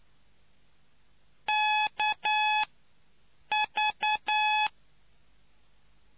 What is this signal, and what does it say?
Morse code, heard as a keyed tone, about 9 wpm, 850 Hz. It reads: K V